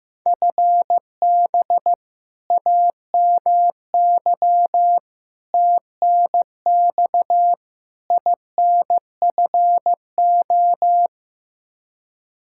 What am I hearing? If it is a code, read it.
Morse code, heard as a keyed tone, about 15 wpm, 700 Hz. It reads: FB AMY TNX INFO
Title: Practice QSO - Exchange 3 - More Info - 15wpm